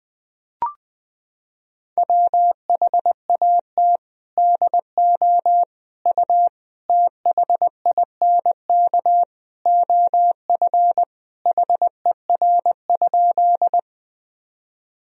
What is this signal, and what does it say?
Morse code, heard as a keyed tone, about 21 wpm, 700 Hz. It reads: WHAT DO U THINK OF HER?
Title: Sentences from Top 200 Words and Abbreviations 20wpm